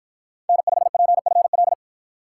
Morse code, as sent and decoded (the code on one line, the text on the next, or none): -. ..... .--. ..-. .-..
N5PFL